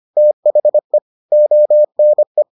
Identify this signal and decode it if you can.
Morse code, heard as a keyed tone, about 25 wpm, 600 Hz. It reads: THE ONE